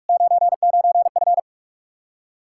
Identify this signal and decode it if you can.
Morse code, heard as a keyed tone, about 54 wpm, 700 Hz. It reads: 99F
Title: Speed-Racing Common QSO Phrases - 45wpm